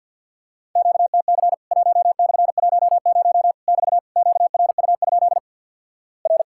Morse code, heard as a keyed tone, about 50 wpm, 700 Hz. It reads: QTC 1/10 <BT> QRU? R
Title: Worked All Europe DX Contest - QTC Exchange - 50wpm